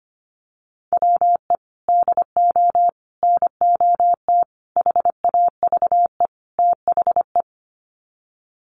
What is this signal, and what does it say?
Morse code, heard as a keyed tone, about 25 wpm, 700 Hz. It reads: WE DO NOT HAVE THE
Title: Sets of 5 Words 25wpm